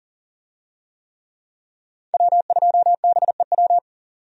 Morse code, heard as a keyed tone, about 40 wpm, 700 Hz. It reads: W2BEW
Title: USA Call Signs - 5 Characters - 40wpm